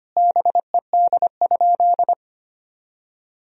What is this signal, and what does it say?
Morse code, heard as a keyed tone, about 25 wpm, 700 Hz. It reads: BED?
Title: Sentences from Top 1000 Words - Review Part 1 - 25wpm